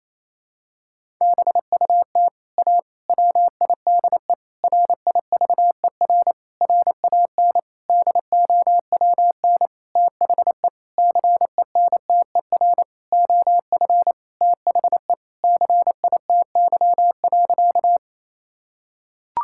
Morse code, heard as a keyed tone, about 28 wpm, 700 Hz. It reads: BUT A WIDE RIVER RAN DOWN THE CENTER OF THE CITY.
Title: Sentences from Top 1000 Words - Review Part 6 - 28wpm